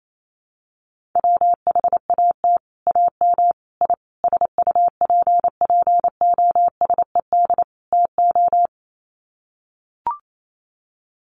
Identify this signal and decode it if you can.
Morse code, heard as a keyed tone, about 28 wpm, 700 Hz. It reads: WHAT AM I SUPPOSED TO
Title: Sets of 5 Words Encore2 28wpm